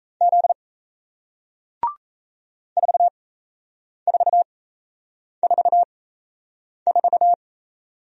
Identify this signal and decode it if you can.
Morse code, heard as a keyed tone, about 38 wpm, spaced as for 8 wpm, 700 Hz. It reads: Z 4444